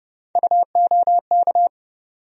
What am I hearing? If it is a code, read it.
Morse code, heard as a keyed tone, about 31 wpm, 700 Hz. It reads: UOK
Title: USA Call Sign Postfixes - 3 Letters 30wpm